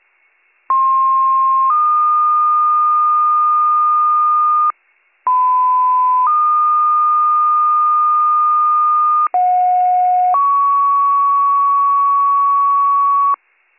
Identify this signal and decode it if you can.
Two-tone page, heard as a one-second tone, then a three-second tone, about 1060 then 1220 Hz, about 1010 then 1220 Hz, about 730 then 1120 Hz. It is heard from 0.7 to 13.4 s.